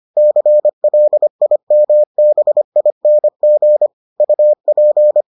Morse code, heard as a keyed tone, about 25 wpm, 600 Hz. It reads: CLIMBING UP